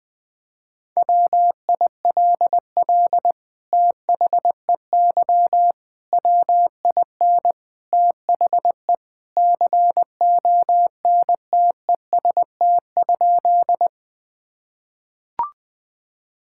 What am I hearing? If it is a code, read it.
Morse code, heard as a keyed tone, about 21 wpm, 700 Hz. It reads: WILL THEY WIN THE CONTEST?
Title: Sentences from Top 1600 Words 20wpm